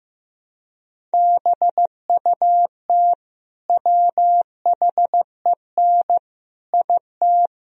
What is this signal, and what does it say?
Morse code, heard as a keyed tone, about 16 wpm, 700 Hz. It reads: BUT WHEN IT
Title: Speed-Racing - Sets of 5 Words 15wpm